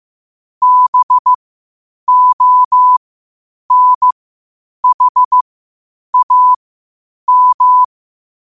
Morse code, heard as a keyed tone, about 15 wpm, spaced as for 8 wpm, 1000 Hz. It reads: BONHAM